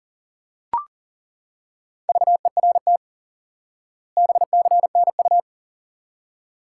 Morse code, heard as a keyed tone, about 40 wpm, 700 Hz. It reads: VERT BCNU
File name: IQE Course 3 - Lesson 89 - YAGI - Two Elements - 40wpm